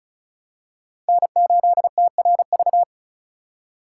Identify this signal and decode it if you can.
Morse code, heard as a keyed tone, about 34 wpm, 700 Hz. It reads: N8TRV
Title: USA Call Signs - 5 Characters - 3x Character Spacing - 35wpm